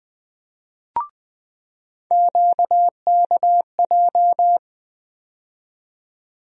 Morse code, heard as a keyed tone, about 20 wpm, 700 Hz. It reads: QKJ
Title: USA Call Sign Postfixes - 3 Letters 20wpm